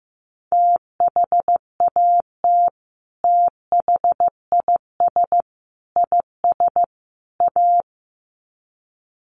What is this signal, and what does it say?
Morse code, heard as a keyed tone, about 15 wpm, 700 Hz. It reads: THAT THIS IS A
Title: Sets of 4 Words 15wpm